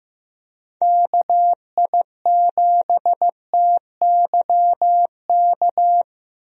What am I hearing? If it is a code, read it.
Morse code, heard as a keyed tone, about 15 wpm, 700 Hz. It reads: KI7TYK
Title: USA Call Signs - 6 Characters - Repeated 2x - 15wpm